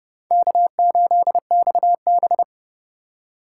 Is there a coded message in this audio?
Yes: Morse code from 0.3 to 2.4 s, heard as a keyed tone, about 29 wpm, 700 Hz. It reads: K8XB